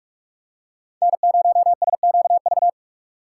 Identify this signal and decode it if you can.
Morse code, heard as a keyed tone, about 43 wpm, 700 Hz. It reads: N0SQV
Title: USA Call Signs - 4x Character Spacing - 45wpm